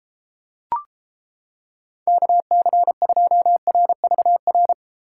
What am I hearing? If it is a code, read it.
Morse code, heard as a keyed tone, about 29 wpm, 700 Hz. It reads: KC2RVR